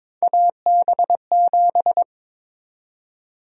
Morse code, heard as a keyed tone, about 21 wpm, 700 Hz. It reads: AB7
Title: USA Call Sign Prefixes - 22wpm